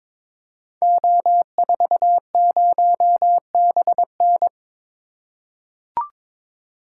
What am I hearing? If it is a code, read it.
Morse code, heard as a keyed tone, about 22 wpm, 700 Hz. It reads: O40BN
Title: Code Groups - 5 Letter:Numbers - 22wpm